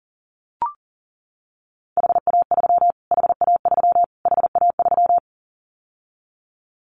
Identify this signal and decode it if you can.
Morse code, heard as a keyed tone, about 40 wpm, 700 Hz. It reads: HA3 HA3 HA3
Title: Callsign Prefixes for Top 25 DXCC - Repeated 3x - 40wpm